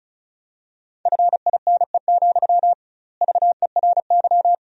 Morse code, heard as a keyed tone, about 34 wpm, 700 Hz. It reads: FINE, VERY
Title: 2x Word-Spacing - Sentences from Top 500 Words - 35wpm